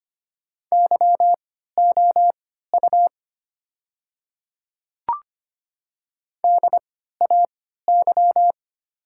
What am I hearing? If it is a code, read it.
Morse code, heard as a keyed tone, about 25 wpm, spaced as for 14 wpm, 700 Hz. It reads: YOU DAY
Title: 3x Character Spacing - Top 100 Words - 25wpm